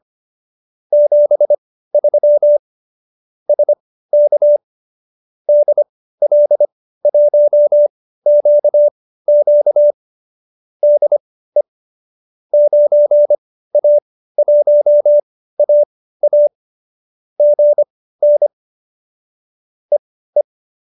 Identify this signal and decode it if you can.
Morse code, heard as a keyed tone, about 25 wpm, spaced as for 15 wpm, 590 Hz. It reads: 73 SK DL1QQ DE 9A1AA GN EE